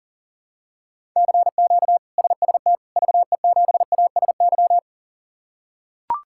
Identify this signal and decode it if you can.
Morse code, heard as a keyed tone, about 40 wpm, 700 Hz. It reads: CQ SST VE7ASY
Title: K1USN SST QSO Practice - 40wpm